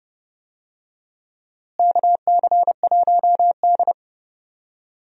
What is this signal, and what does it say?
Morse code, heard as a keyed tone, about 30 wpm, 700 Hz. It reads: KC1D
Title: USA Call Signs - 4 Characters - Repeated 2x - 30wpm